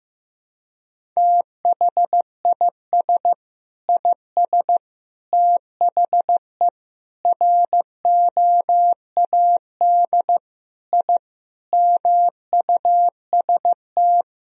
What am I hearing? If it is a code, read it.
Morse code, heard as a keyed tone, about 15 wpm, 700 Hz. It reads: THIS IS THE ROAD I MUST
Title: Sentences from Top 1000 Words - Review Part 2 - 15wpm